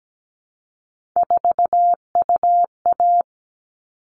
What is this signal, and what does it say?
Morse code, heard as a keyed tone, about 17 wpm, 700 Hz. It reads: 4UA
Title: All Cut Numbers - 3-Digit 17wpm